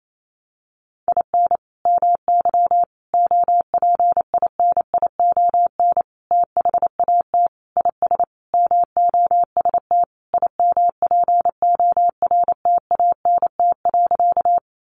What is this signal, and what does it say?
Morse code, heard as a keyed tone, about 28 wpm, 700 Hz. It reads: IN MY OPINION THAT IS MOST IMPORTANT.
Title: Sentences from Top 1200 Words 28wpm